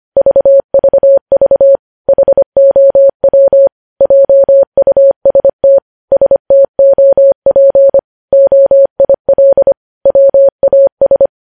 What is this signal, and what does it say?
Morse code, heard as a keyed tone, about 25 wpm, 550 Hz. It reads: VVV HOW JUST STOP OIL WAS
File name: podcast / News headlines-1x-25x25